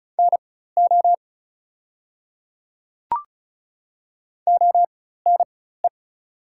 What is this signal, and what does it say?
Morse code, heard as a keyed tone, about 36 wpm, spaced as for 16 wpm, 700 Hz. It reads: NO ONE
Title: 4x Character Spacing - Top 100 Words - 35wpm